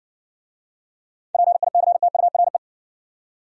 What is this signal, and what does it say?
Morse code, heard as a keyed tone, about 60 wpm, 700 Hz. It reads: PICTURE